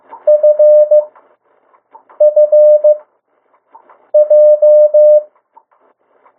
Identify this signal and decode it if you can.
Morse code, heard as a keyed tone, about 15 wpm, spaced as for 6 wpm, 600 Hz. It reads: FFJ
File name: morse mp3 / 02 urok 1